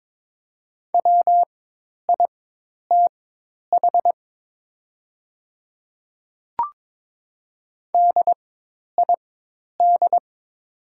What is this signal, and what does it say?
Morse code, heard as a keyed tone, about 22 wpm, spaced as for 10 wpm, 700 Hz. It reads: WITH DID
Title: 4x Character Spacing - Top 100 Words - 22wpm